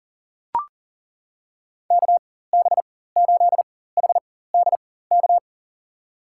Morse code, heard as a keyed tone, about 40 wpm, spaced as for 19 wpm, 700 Hz. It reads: KB8HDK
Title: USA Call Signs - 6 Characters - 4x Character Spacing - 40wpm